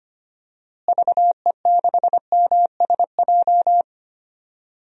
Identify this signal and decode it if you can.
Morse code, heard as a keyed tone, about 25 wpm, 700 Hz. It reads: VE6MSJ
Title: DXCC-100 Call Signs - 6 Characters - Repeated 3x - 25wpm